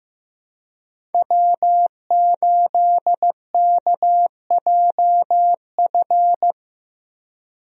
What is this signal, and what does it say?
Morse code, heard as a keyed tone, about 15 wpm, 700 Hz. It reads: W8KJF
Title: USA Call Signs - 5 Characters - Repeated 3x - 15wpm